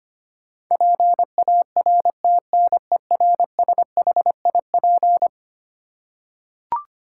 Morse code, heard as a keyed tone, about 25 wpm, 700 Hz. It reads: PARTNERSHIP
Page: morse-code-ninja-practice-25wpm